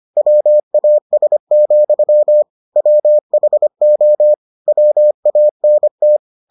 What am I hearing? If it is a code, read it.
Morse code, heard as a keyed tone, about 25 wpm, 600 Hz. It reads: WAS, WHO WANT